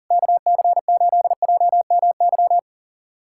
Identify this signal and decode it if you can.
Morse code, heard as a keyed tone, about 40 wpm, 700 Hz. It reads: KC8JMY
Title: USA Callsign DE Callsign - 6 Characters - 40wpm